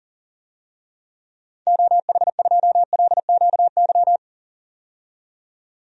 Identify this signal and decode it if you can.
Morse code, heard as a keyed tone, about 40 wpm, 700 Hz. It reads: OH2LQY
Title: DXCC-100 Call Signs - 6 Characters - 40wpm